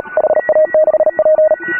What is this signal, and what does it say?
Morse code, heard as a keyed tone, about 38 wpm, 590 Hz. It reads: HA6P
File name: AUDIO2018 / HA3FMR CW